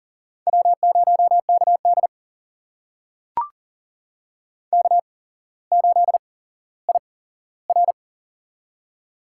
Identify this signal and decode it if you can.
Morse code, heard as a keyed tone, about 40 wpm, spaced as for 17 wpm, 700 Hz. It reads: W0KD K8IR
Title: USA Call Signs - 4 Characters - 8x Character Spacing - 40wpm